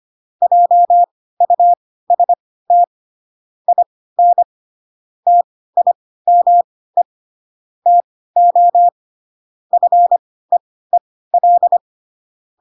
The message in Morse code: JUST IN TIME TO FEEL

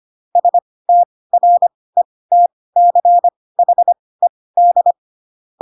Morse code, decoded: STRETCHED

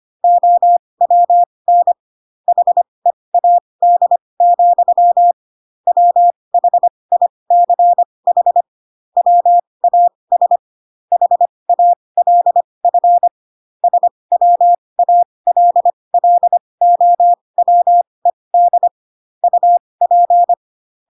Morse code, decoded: OWN HEAD, WHICH WAS HALF SWALLOWED UP

